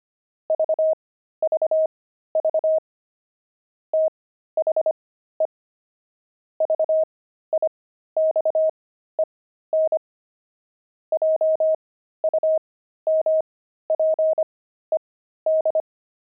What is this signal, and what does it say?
Morse code, heard as a keyed tone, about 25 wpm, spaced as for 13 wpm, 630 Hz. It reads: VVV THE VIXEN JUMPED
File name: pangram-fs-13-cs-25-barking